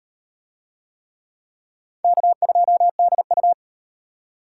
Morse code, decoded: K2DU

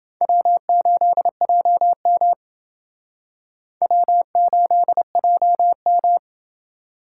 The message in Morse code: W8JM W8JM

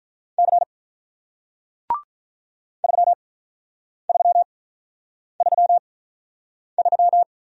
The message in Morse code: C 3333

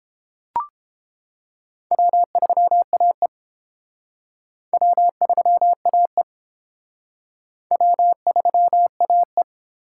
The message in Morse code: W3AE W3AE W3AE

